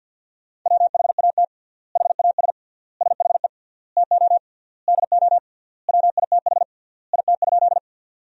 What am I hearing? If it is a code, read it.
Morse code, heard as a keyed tone, about 53 wpm, 700 Hz. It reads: WHAT HAS SHE TO DO WITH IT?
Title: 3x Word-Spacing - Sentences from Top 100 Words - 50wpm